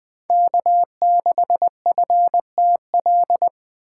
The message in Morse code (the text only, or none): K6FTL